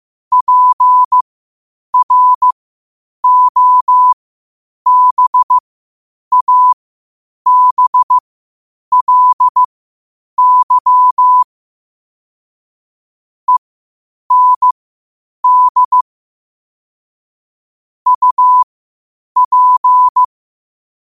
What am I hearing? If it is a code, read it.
Morse code, heard as a keyed tone, about 15 wpm, spaced as for 8 wpm, 1000 Hz. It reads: PROBABLY END UP
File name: qotdfiles / 8wpm